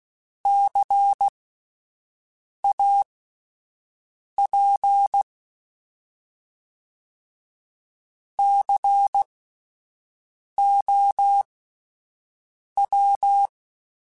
Morse code, heard as a keyed tone, about 16 wpm, spaced as for 5 wpm, 800 Hz. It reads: CAP COW
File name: Spaced Alphas / Spaced Alphas mp3s / Cs sp @10